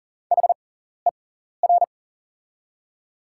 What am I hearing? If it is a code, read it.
Morse code, heard as a keyed tone, about 40 wpm, spaced as for 14 wpm, 690 Hz. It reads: HER
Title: 6x Character Spacing - Top 300 Words - 40wpm